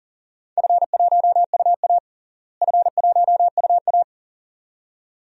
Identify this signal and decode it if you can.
Morse code, heard as a keyed tone, about 40 wpm, 700 Hz. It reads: F1UA F1UA
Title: DXCC-100 Call Signs - 4 Characters - Repeated 2x - 40wpm